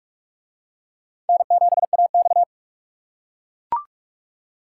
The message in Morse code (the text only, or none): N7AX